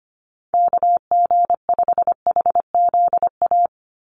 Morse code, heard as a keyed tone, about 25 wpm, 700 Hz. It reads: KG5HZA
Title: USA Call Signs 25wpm